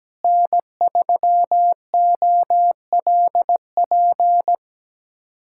Morse code, heard as a keyed tone, about 17 wpm, 700 Hz. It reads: N3OLP
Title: K1USN SST QSO Practice - 17wpm